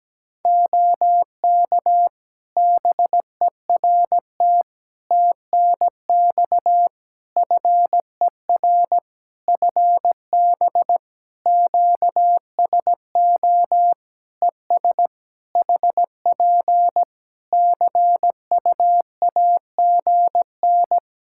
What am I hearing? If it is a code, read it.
Morse code, heard as a keyed tone, about 17 wpm, 700 Hz. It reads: OK BERT TNX FER FB QSO ES HP CUAGN
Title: Practice QSO - Exchange 4 - Ending - 17wpm